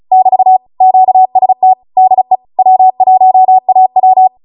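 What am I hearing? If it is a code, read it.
Morse code, heard as a keyed tone, about 36 wpm, 750 Hz. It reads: <BT> QST DE W1AW
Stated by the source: Amateur radio Morse code practice archive transmission at 35 wpm.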